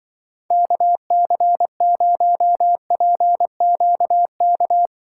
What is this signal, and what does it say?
Morse code, heard as a keyed tone, about 25 wpm, 700 Hz. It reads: KC0PQK
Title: Speed Racing - USA Call Signs 20wpm